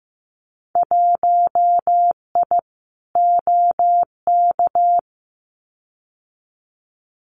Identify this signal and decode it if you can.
Morse code, heard as a keyed tone, about 15 wpm, 700 Hz. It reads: 1I OK
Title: Winter Field Day Exchanges Encore 15wpm